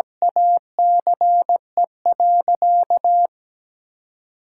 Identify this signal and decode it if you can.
Morse code, heard as a keyed tone, about 17 wpm, 700 Hz. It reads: ACE.
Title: Sentences from Top 1000 Words - Review Part 5 - 17wpm